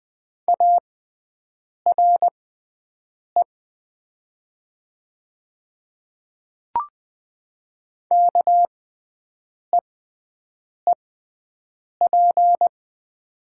Morse code, heard as a keyed tone, about 20 wpm, spaced as for 7 wpm, 700 Hz. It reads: ARE KEEP
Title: 6x Character Spacing - Top 300 Words - 20wpm